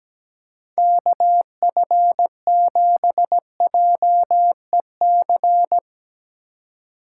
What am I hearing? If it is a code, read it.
Morse code, heard as a keyed tone, about 17 wpm, 700 Hz. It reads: KF7JEC
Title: Speed Racing - USA 2x3 Call Signs 17wpm